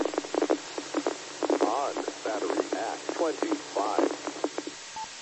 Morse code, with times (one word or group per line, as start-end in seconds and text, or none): none